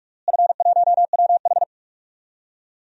Morse code, heard as a keyed tone, about 41 wpm, 700 Hz. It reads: F1WH
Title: DXCC-100 Call Signs - 4 Characters - Repeated 3x - 45wpm